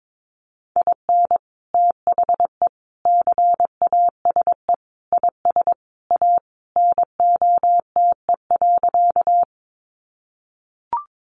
Morse code, heard as a keyed tone, about 22 wpm, 700 Hz. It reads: IN THE CASE IS A NOTE.
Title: Sentences from Top 700 Words 22wpm